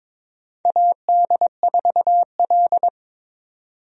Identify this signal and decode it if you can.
Morse code, heard as a keyed tone, about 21 wpm, 700 Hz. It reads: AD4L
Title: USA Call Signs - 4 Characters - 22wpm